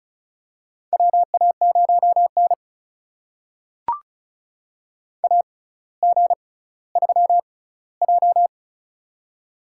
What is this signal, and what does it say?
Morse code, heard as a keyed tone, about 35 wpm, spaced as for 18 wpm, 700 Hz. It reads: WA0N AG3J
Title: USA Call Signs - 4 Characters - 6x Character Spacing - 35wpm